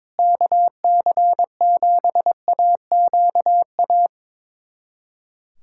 Morse code, heard as a keyed tone, about 21 wpm, 700 Hz. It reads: KC7AQA
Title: USA Callsign DE Callsign - 6 Characters - 22wpm